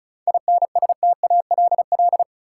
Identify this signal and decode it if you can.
Morse code, heard as a keyed tone, about 36 wpm, 700 Hz. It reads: INSTALL
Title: Sentences from Top 1600 Words - New Word Drill Alt - 35wpm